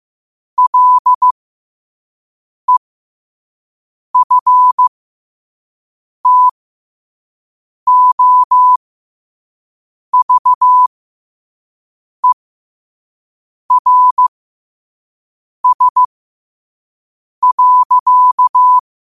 Morse code, heard as a keyed tone, about 15 wpm, spaced as for 5 wpm, 1000 Hz. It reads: LEFTOVERS.
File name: qotdfiles / 5wpm